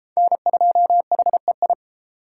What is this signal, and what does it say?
Morse code, heard as a keyed tone, about 31 wpm, 700 Hz. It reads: N2HEI